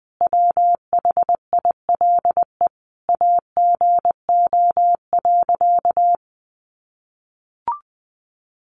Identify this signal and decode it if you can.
Morse code, heard as a keyed tone, about 20 wpm, 700 Hz. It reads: WHILE AGO.
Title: Sentences from Top 600 Words 20wpm